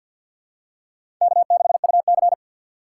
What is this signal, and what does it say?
Morse code, heard as a keyed tone, about 54 wpm, 700 Hz. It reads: K6UC
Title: USA Call Signs - 3x Character Spacing - 50wpm